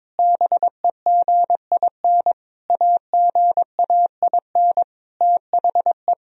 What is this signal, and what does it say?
Morse code, heard as a keyed tone, about 22 wpm, 700 Hz. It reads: BEGIN AGAIN THE
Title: Sentences from Top 1000 Words - Review Part 1 - 22wpm